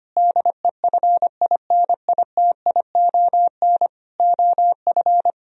Morse code, heard as a keyed tone, about 25 wpm, 700 Hz. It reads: DEFINITION OF
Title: Sentences from Top 1000 Words - Review Part 4 - 25wpm